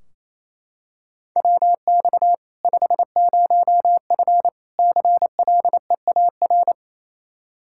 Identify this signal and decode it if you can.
Morse code, heard as a keyed tone, about 28 wpm, 700 Hz. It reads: WX 50F CLEAR
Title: K1USN SST QSO Practice 28wpm